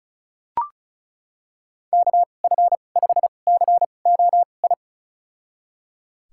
Morse code, heard as a keyed tone, about 36 wpm, spaced as for 25 wpm, 700 Hz. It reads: KF5COI
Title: USA Call Signs - 6 Characters - 2x Character Spacing - 35wpm